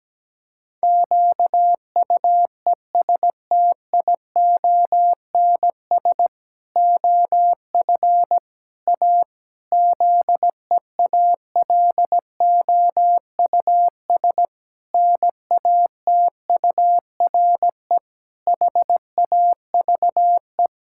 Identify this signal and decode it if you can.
Morse code, heard as a keyed tone, about 17 wpm, 700 Hz. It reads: QUESTIONS OF A ZEALOUS NATURE HAVE
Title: Pangram Challenge v2 - 17wpm